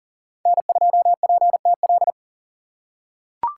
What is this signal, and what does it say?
Morse code, heard as a keyed tone, about 40 wpm, 700 Hz. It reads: N2PTL
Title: USA Call Signs - 5 Characters - 2x Character Spacing - 40wpm